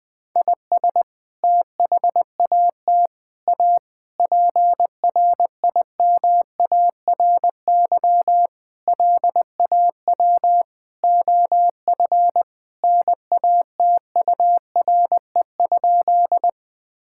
Morse code, heard as a keyed tone, about 20 wpm, 700 Hz. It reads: IS THAT A PRIMARY LAW OF NATURE?
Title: Sentences from Top 1000 Words - Review Part 5 - 20wpm